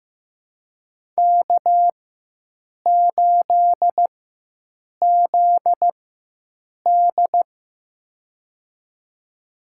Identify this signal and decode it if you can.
Morse code, heard as a keyed tone, about 16 wpm, spaced as for 7 wpm, 700 Hz. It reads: K8ZD
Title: USA Call Signs - 4 Characters - 4x Character Spacing - 15wpm